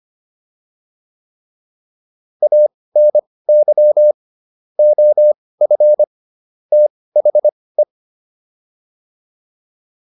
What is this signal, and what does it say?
Morse code, heard as a keyed tone, about 25 wpm, spaced as for 18 wpm, 600 Hz. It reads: ANY OF THE
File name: WD204 18